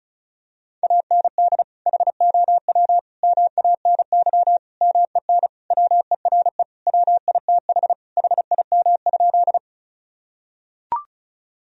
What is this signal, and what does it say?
Morse code, heard as a keyed tone, about 34 wpm, 700 Hz. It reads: AND HOW MANY MEN WERE WITH HIM?
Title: Sentences from Top 1000 Words - Review Part 1 - 35wpm